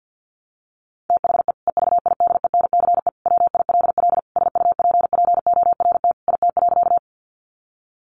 Morse code, heard as a keyed tone, about 50 wpm, 700 Hz. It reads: THE EVIDENCE WILL SUPPORT IT.
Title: Sentences from Top 1300 Words 50wpm